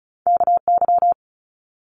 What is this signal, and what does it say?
Morse code, heard as a keyed tone, about 34 wpm, 700 Hz. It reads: KY
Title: US States Repeated 3x 35wpm